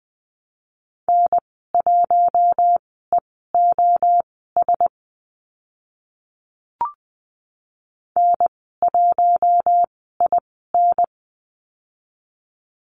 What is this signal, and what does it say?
Morse code, heard as a keyed tone, about 20 wpm, spaced as for 14 wpm, 700 Hz. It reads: N1EOS N1IN
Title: 042 Lesson 8g - Number 1 - Call Signs 20wpm